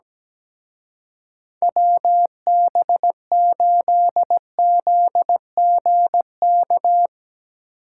Morse code, heard as a keyed tone, about 18 wpm, 700 Hz. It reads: WB8ZGK
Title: Speed Racing - USA 2x3 Call Signs 17wpm